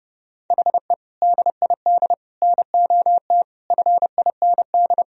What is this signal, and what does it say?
Morse code, heard as a keyed tone, about 30 wpm, 700 Hz. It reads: HE DID NOT FIND
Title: Sentences from Top 1000 Words - Review Part 3 - 30wpm